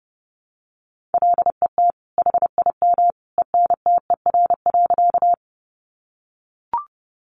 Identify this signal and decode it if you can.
Morse code, heard as a keyed tone, about 30 wpm, 700 Hz. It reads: LET HIM ENTER.
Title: Sentences from Top 1300 Words 30wpm